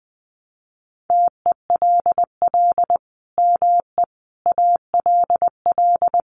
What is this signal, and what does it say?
Morse code, heard as a keyed tone, about 20 wpm, 700 Hz. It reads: TELL ME ALL
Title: Sentences from Top 200 Words 20wpm